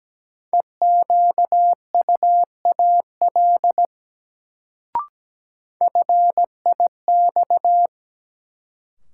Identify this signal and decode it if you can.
Morse code, heard as a keyed tone, about 17 wpm, 700 Hz. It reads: EQUAL FIX
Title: Sentences from Top 1400 Words - New Word Drill - 17wpm